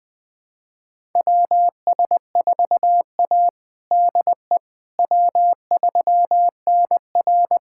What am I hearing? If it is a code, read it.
Morse code, heard as a keyed tone, about 20 wpm, 700 Hz. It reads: WS4A DE W3NR
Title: USA Callsign DE Callsign - 4 Characters - 20wpm